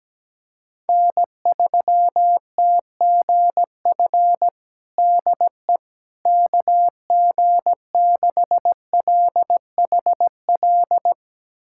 Morse code, heard as a keyed tone, about 17 wpm, 700 Hz. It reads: N3TGF DE KG6LHL
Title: USA Callsign DE Callsign 17wpm